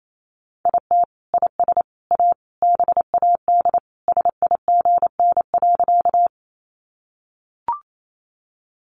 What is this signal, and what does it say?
Morse code, heard as a keyed tone, about 28 wpm, 700 Hz. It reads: IT IS A BAD SIGN.
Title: Sentences from Top 1000 Words 28wpm